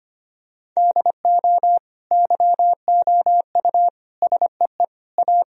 Morse code, heard as a keyed tone, about 25 wpm, 700 Hz. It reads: DO YOU SEE A
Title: Sentences from Top 1000 Words - Review Part 4 - 25wpm